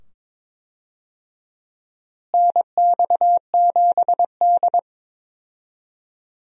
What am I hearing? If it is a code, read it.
Morse code, heard as a keyed tone, about 22 wpm, 700 Hz. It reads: NX7D
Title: USA Call Signs - 4 Characters - Repeated 2x - 22wpm